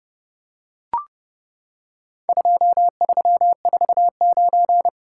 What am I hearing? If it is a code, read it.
Morse code, heard as a keyed tone, about 32 wpm, 700 Hz. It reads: 2349
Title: Worked All Europe DX Contest - QTC Exchange - 30wpm